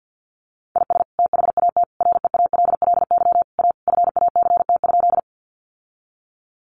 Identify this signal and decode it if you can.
Morse code, heard as a keyed tone, about 50 wpm, 700 Hz. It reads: IS THAT REALLY A FACT?
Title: Sentences from Top 600 Words 50wpm